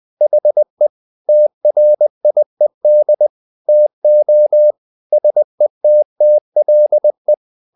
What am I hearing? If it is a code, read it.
Morse code, heard as a keyed tone, about 20 wpm, 600 Hz. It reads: HE TRIED TO SETTLE
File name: the old man and the sea part 2 20wpm